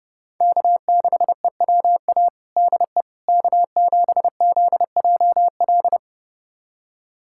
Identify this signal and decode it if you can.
Morse code, heard as a keyed tone, about 32 wpm, 700 Hz. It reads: K6EWA DE K7ZJL
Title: USA Callsign DE Callsign - 5 Characters - 30wpm